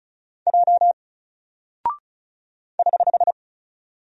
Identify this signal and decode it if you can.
Morse code, heard as a keyed tone, about 36 wpm, 700 Hz. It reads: J <HH>
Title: Single Character (Letter:Number:Punctuation:Prosign) Alt 35wpm